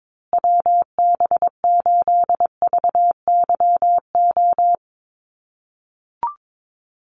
Morse code, heard as a keyed tone, about 22 wpm, 700 Hz. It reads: WB8VYO
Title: USA Call Signs Repeated 3x 22wpm